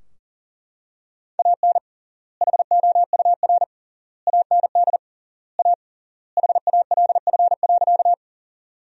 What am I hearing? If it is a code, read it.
Morse code, heard as a keyed tone, about 40 wpm, 700 Hz. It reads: AN HOUR AND A HALF.
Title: 3x Word-Spacing - Sentences from Top 400 Words - 40wpm